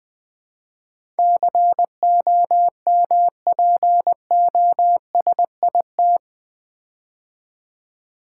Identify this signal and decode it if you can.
Morse code, heard as a keyed tone, about 20 wpm, 700 Hz. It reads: COMPOSIT
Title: Build A Word - 4 Letters At A Time - 11-Letter Words - 20wpm